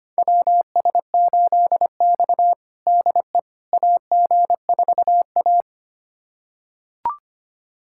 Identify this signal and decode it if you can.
Morse code, heard as a keyed tone, about 25 wpm, 700 Hz. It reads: WS8X DE AG4A
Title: USA Callsign DE Callsign - 4 Characters - 25wpm